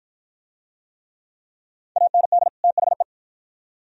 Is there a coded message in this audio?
Yes: Morse code from 2.0 to 3.0 s, heard as a keyed tone, about 46 wpm, 690 Hz. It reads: AND THE